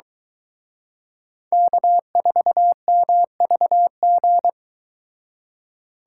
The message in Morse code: K4MVG

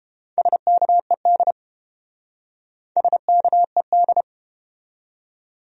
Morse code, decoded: SKED SKED

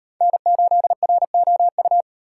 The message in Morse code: N8ROU

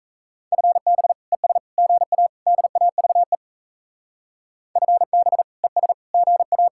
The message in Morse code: FB ES GA DAVE FB ES GA

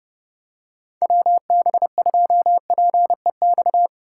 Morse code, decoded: WB2PEX

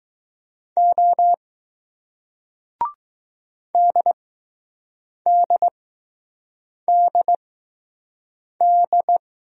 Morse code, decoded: O DDDD

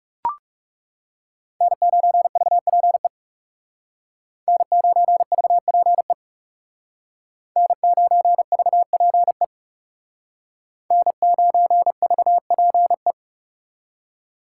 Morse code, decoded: N9VPE N9VPE N9VPE N9VPE

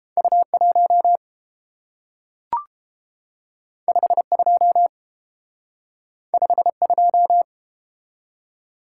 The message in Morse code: U1 52 52

